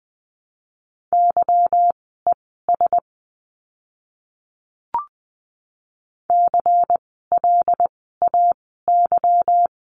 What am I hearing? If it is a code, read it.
Morse code, heard as a keyed tone, about 20 wpm, spaced as for 14 wpm, 700 Hz. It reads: YES CLAY
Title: 146 Lesson 24f - Letter y - Words 20wpm